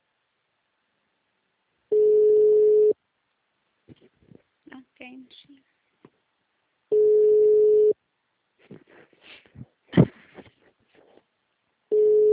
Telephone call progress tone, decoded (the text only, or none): ringback tone